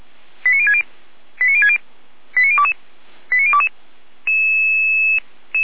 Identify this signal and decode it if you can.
Five-tone selective call (ZVEI), heard as quick runs of five tones, tones of about 70 ms. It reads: 89060 89060 89022 89022
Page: feuerwehr-orlamuende